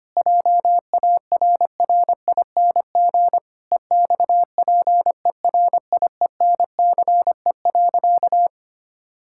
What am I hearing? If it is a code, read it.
Morse code, heard as a keyed tone, about 25 wpm, 700 Hz. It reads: JARRING EXPERIENCE.
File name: practice / 25wpm / All-Age Appropriate Jokes - 25wpm